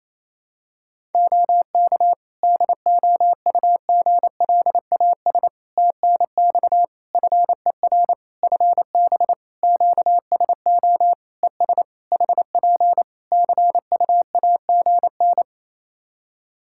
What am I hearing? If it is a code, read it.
Morse code, heard as a keyed tone, about 28 wpm, 700 Hz. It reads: OK DOUGLAS TNX FER FB QSO ES HP CUAGN
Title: Practice QSO - Exchange 4 - Ending - 28wpm